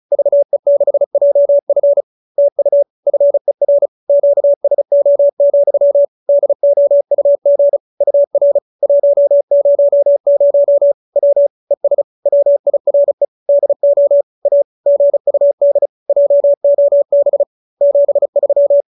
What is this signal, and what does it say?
Morse code, heard as a keyed tone, about 34 wpm, 570 Hz. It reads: VE6JF TU FER QSO, DOUG UR 100 W ES WIRE DO A GUD JOB 73